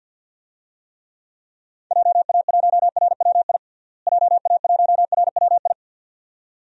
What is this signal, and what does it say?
Morse code, heard as a keyed tone, about 53 wpm, 700 Hz. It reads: JA1RWI JA1RWI
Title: USA Call Signs - 6 Characters - Repeated 2x - 50wpm